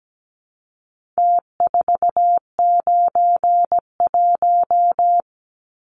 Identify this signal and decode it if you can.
Morse code, heard as a keyed tone, about 17 wpm, 700 Hz. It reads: T491
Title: Numbers - Contest Style - 4-Digit 17wpm